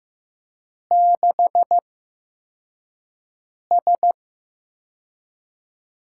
Morse code, heard as a keyed tone, about 15 wpm, spaced as for 4 wpm, 700 Hz. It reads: 6S
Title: USA Call Signs - 8x Character Spacing - 15wpm